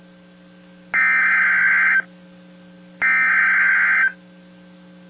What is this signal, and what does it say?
Emergency Alert System (SAME) tones (header bursts), heard as harsh buzzing bursts. It is heard from 1.0 to 4.0 s.